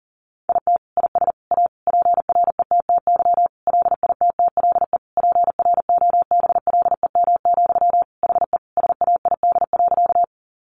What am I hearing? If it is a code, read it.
Morse code, heard as a keyed tone, about 40 wpm, 700 Hz. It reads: IT IS A PRETTY LITTLE PROBLEM, HE SAID.
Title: Sentences from Top 1000 Words 40wpm